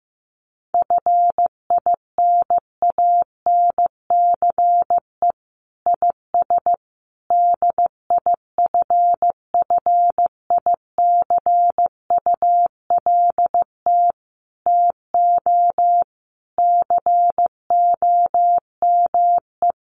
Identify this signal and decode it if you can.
Morse code, heard as a keyed tone, about 15 wpm, 700 Hz. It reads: FINANCE IS DIFFICULT TO COME
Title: Sentences from Top 1500 Words 15wpm